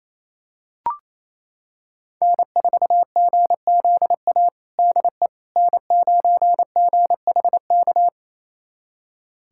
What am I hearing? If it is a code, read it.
Morse code, heard as a keyed tone, about 28 wpm, 700 Hz. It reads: N4GZA DE N9GHK
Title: USA Callsign DE Callsign - 5 Characters - 28wpm